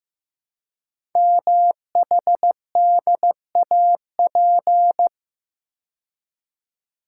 Morse code, heard as a keyed tone, about 15 wpm, 700 Hz. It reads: MHDAP
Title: Code Groups - 5 Letters - 15wpm